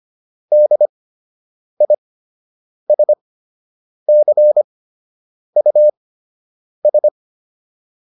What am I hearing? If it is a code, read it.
Morse code, heard as a keyed tone, about 25 wpm, spaced as for 8 wpm, 600 Hz. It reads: DISCUS
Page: fundamental-practice-files